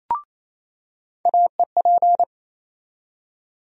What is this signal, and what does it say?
Morse code, heard as a keyed tone, about 28 wpm, 700 Hz. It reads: AEP
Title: USA Call Sign Postfixes - 3 Letters 28wpm